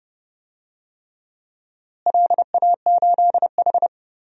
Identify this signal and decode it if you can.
Morse code, heard as a keyed tone, about 30 wpm, 700 Hz. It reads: LA8H
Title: DXCC-100 Call Signs - 4 Characters - Repeated 3x - 30wpm